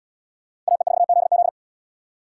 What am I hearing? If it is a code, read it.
Morse code, heard as a keyed tone, about 77 wpm, 700 Hz. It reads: AE4ZZ